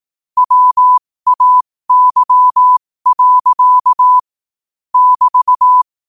Morse code, heard as a keyed tone, about 18 wpm, spaced as for 15 wpm, 1000 Hz. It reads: WAY. <BT>
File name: qotdfiles / 15wpm